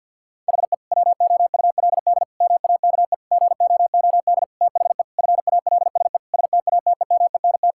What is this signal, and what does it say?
Morse code, heard as a keyed tone, about 52 wpm, 700 Hz. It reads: HE WOULD MAKE GOOD THE FALSE STATEMENT
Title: Sentences from Top 1600 Words 50wpm